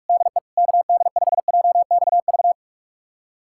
Morse code, heard as a keyed tone, about 46 wpm, 690 Hz. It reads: DE KD5JXV
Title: USA Callsign DE Callsign - 6 Characters - 45wpm